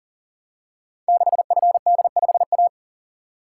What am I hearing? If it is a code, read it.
Morse code, heard as a keyed tone, about 39 wpm, 700 Hz. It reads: 6FD5A
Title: Code Groups - 5 Letter:Numbers - 40wpm